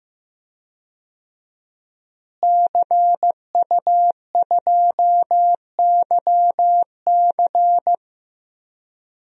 Morse code, heard as a keyed tone, about 15 wpm, 700 Hz. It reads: CU2YC
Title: DXCC-100 Call Signs - 5 Characters - Repeated 2x - 15wpm